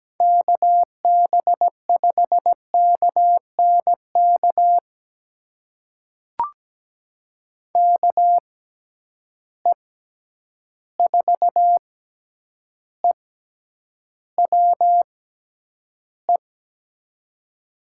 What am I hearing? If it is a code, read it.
Morse code, heard as a keyed tone, about 17 wpm, spaced as for 9 wpm, 700 Hz. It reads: KB5KNK KE4EWE
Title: USA Call Signs - 6 Characters - 6x Character Spacing - 17wpm